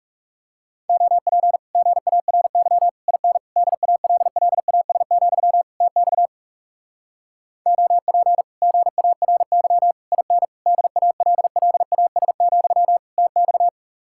Morse code, heard as a keyed tone, about 40 wpm, 700 Hz. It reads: OP GARY IN DALLAS, TX OP GARY IN DALLAS, TX